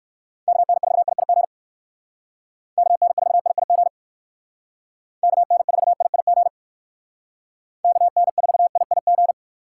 Morse code, heard as a keyed tone, about 61 wpm, 700 Hz. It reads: KN4IIG KN4IIG KN4IIG KN4IIG